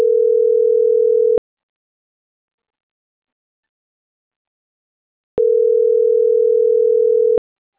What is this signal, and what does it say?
Ringback tone (North America), 440 + 480 Hz.